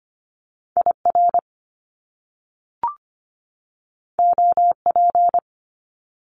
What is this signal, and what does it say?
Morse code, heard as a keyed tone, about 25 wpm, 700 Hz. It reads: I R O P
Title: Top Two Letter Combinations in Words 25wpm